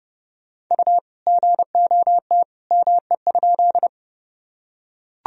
Morse code, heard as a keyed tone, about 30 wpm, 700 Hz. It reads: U GOT ME?
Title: Sentences from Top 300 Words and Abbreviations 30wpm